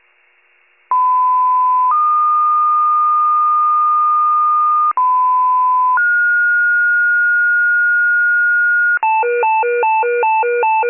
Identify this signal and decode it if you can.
Two-tone page, heard as a one-second tone, then a three-second tone, about 1010 then 1220 Hz, about 1010 then 1430 Hz, with noise. It is heard from 0.9 to 9.0 s.